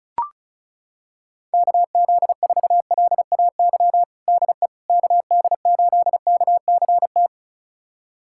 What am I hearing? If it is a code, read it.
Morse code, heard as a keyed tone, about 36 wpm, 700 Hz. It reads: KZ4LAY DE KD8KCT
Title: USA Callsign DE Callsign - 6 Characters - 35wpm